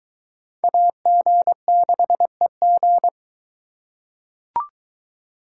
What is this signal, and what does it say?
Morse code, heard as a keyed tone, about 21 wpm, 700 Hz. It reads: AG6EG